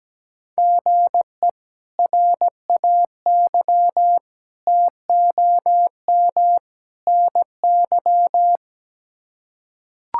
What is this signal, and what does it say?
Morse code, heard as a keyed tone, about 17 wpm, 700 Hz. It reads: GE RAY TOM NY
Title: K1USN SST QSO Practice - 17wpm